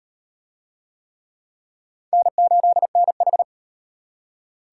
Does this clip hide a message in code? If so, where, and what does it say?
Yes: Morse code from 2.1 to 3.4 s, heard as a keyed tone, about 38 wpm, 700 Hz. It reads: N8NH